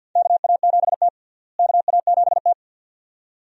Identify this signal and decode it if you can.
Morse code, heard as a keyed tone, about 52 wpm, 700 Hz. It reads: KA7T KA7T
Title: USA Call Signs - 4 Characters - Repeated 2x - 50wpm